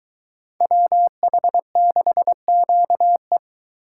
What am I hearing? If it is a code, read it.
Morse code, heard as a keyed tone, about 22 wpm, 700 Hz. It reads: WH6QE